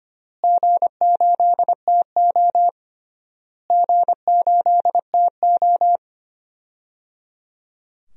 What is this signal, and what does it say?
Morse code, heard as a keyed tone, about 25 wpm, 700 Hz. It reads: G8TO G8TO
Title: DXCC-100 Call Signs - 4 Characters - Repeated 3x - 25wpm